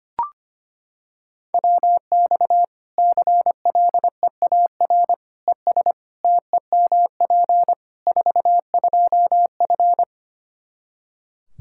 Morse code, heard as a keyed tone, about 25 wpm, 700 Hz. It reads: WX CLEAR ES TEMP 42F
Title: Practice QSO - Exchange 2 - Info - 25wpm